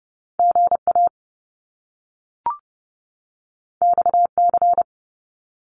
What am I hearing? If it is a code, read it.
Morse code, heard as a keyed tone, about 30 wpm, 700 Hz. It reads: GA XC
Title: Top Two Letter Combinations in Words 30wpm